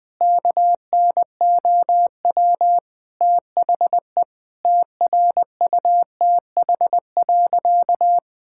Morse code, heard as a keyed tone, about 20 wpm, 700 Hz. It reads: KNOW THE TRUTH.
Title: Sentences with 7-Word Phrases - 20wpm